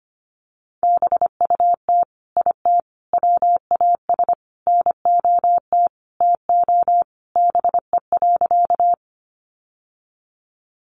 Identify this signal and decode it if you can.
Morse code, heard as a keyed tone, about 25 wpm, 700 Hz. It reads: BUT IT WAS NOT TO BE.
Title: Sentences from Top 600 Words 25wpm